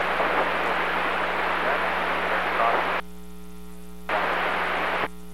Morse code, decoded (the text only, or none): none